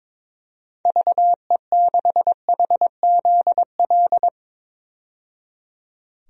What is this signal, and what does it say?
Morse code, heard as a keyed tone, about 22 wpm, 700 Hz. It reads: VE6HZL
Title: DXCC-100 Call Signs - 6 Characters - 22wpm